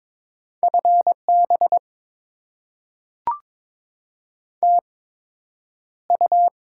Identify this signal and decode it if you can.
Morse code, heard as a keyed tone, about 22 wpm, spaced as for 9 wpm, 700 Hz. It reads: FB TU
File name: Top 20 QSO Elements - 8x Character Spacing - 22wpm